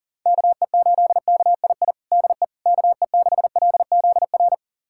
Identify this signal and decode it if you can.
Morse code, heard as a keyed tone, about 37 wpm, 700 Hz. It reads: KE8KII DE KE6LZR
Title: USA Callsign DE Callsign - 6 Characters - 40wpm